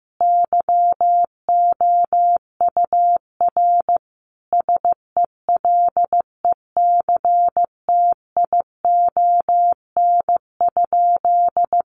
Morse code, heard as a keyed tone, about 15 wpm, 700 Hz. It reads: YOUR SELECTION?
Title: Sentences from Top 1400 Words 15wpm